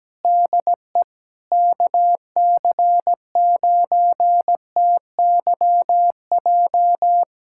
Morse code, heard as a keyed tone, about 17 wpm, 700 Hz. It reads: DE KC9TYJ
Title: USA Callsign DE Callsign - 6 Characters - 17wpm